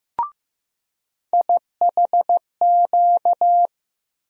Morse code, heard as a keyed tone, about 15 wpm, 700 Hz. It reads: IHQ
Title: USA Call Sign Postfixes - 3 Letters 15wpm